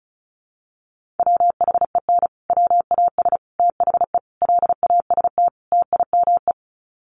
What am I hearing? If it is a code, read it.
Morse code, heard as a keyed tone, about 34 wpm, 700 Hz. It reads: WHEN WAS THE LAST TIME
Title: Sets of 5 Words Encore 35wpm